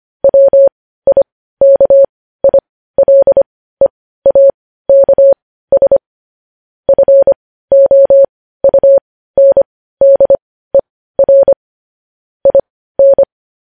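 Morse code, heard as a keyed tone, about 25 wpm, spaced as for 15 wpm, 550 Hz. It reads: WIKILEAKS FOUNDER IN
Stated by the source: Morse news headlines practice at 15.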